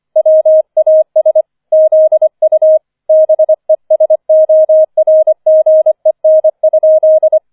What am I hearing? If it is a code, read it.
Morse code, heard as a keyed tone, about 25 wpm, 630 Hz. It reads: WAS ZU BESORGEN?
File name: 005 Fünfter Streich 25wpm